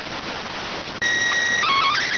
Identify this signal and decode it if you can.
SSTV (slow-scan television), PD 290.